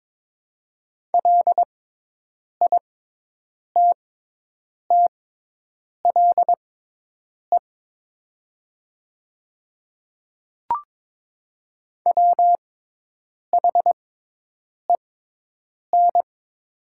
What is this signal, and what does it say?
Morse code, heard as a keyed tone, about 22 wpm, spaced as for 8 wpm, 700 Hz. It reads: LITTLE WHEN
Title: 6x Character Spacing - Top 200 Words - 22wpm